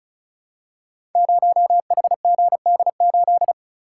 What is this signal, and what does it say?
Morse code, heard as a keyed tone, about 36 wpm, 700 Hz. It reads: 0HGD8
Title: Code Groups - 5 Letter:Numbers - 35wpm